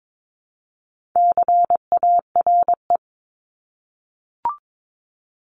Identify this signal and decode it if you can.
Morse code, heard as a keyed tone, about 22 wpm, 700 Hz. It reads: CARE
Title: Multi-letter Phonemes 22wpm